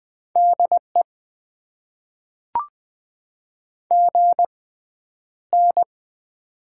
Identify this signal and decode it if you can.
Morse code, heard as a keyed tone, about 20 wpm, spaced as for 10 wpm, 700 Hz. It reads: DE GN
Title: Top 90 QSO Elements - 6x Character Spacing - 20wpm